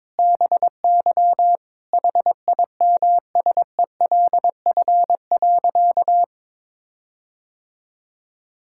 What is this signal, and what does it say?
Morse code, heard as a keyed tone, about 22 wpm, 700 Hz. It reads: BY HIMSELF.
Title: Sentences from Top 1600 Words 22wpm